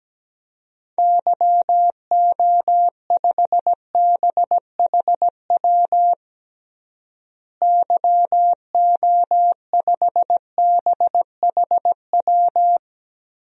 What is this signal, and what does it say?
Morse code, heard as a keyed tone, about 17 wpm, 700 Hz. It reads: YO5BHW YO5BHW
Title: DXCC-100 Call Signs - 6 Characters - Repeated 2x - 17wpm